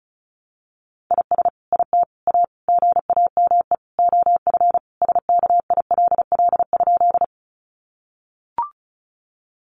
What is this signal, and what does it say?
Morse code, heard as a keyed tone, about 34 wpm, 700 Hz. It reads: IS IT A GAME OF SKILL?
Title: Sentences from Top 1200 Words 35wpm